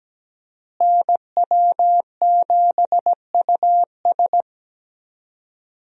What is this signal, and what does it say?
Morse code, heard as a keyed tone, about 17 wpm, 700 Hz. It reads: NW7US
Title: USA Call Signs - 5 Characters - Repeated 3x - 17wpm